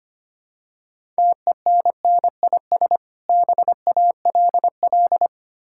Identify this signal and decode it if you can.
Morse code, heard as a keyed tone, about 25 wpm, 700 Hz. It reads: TENNIS BALL
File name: practice / 25wpm / Difficult Words with Hints - 25wpm